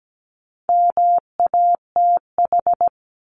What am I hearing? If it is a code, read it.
Morse code, heard as a keyed tone, about 17 wpm, 700 Hz. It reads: MATH
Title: Top 751-1000 Words 17wpm